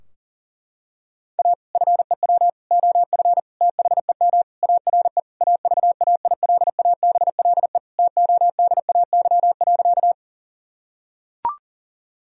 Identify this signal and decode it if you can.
Morse code, heard as a keyed tone, about 40 wpm, 700 Hz. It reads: A FEW OF THEM ARE AVAILABLE TODAY.
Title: Sentences from Top 1000 Words - Review Part 3 - 40wpm